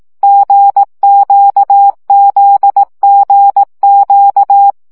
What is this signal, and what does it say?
Morse code, heard as a keyed tone, about 18 wpm, 800 Hz. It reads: GQZGQ
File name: CW gyakorlas g z q 15 baud